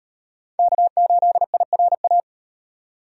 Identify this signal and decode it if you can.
Morse code, heard as a keyed tone, about 38 wpm, 700 Hz. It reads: K8IRA